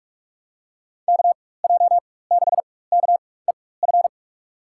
Morse code, heard as a keyed tone, about 46 wpm, spaced as for 21 wpm, 700 Hz. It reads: KJ6KEF
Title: USA Call Signs - 6 Characters - 4x Character Spacing - 45wpm